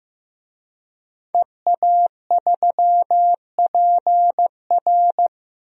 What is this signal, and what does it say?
Morse code, heard as a keyed tone, about 15 wpm, 700 Hz. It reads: EA3PR
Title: DXCC-100 Call Signs - 5 Characters - 15wpm